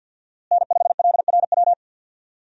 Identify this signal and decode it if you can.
Morse code, heard as a keyed tone, about 52 wpm, 700 Hz. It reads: N5LRW
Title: USA Call Signs - 5 Characters - Repeated 2x - 50wpm